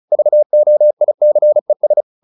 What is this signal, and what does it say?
Morse code, heard as a keyed tone, about 34 wpm, 600 Hz. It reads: VOICES
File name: the old man and the sea / the old man and the sea 1 35wpm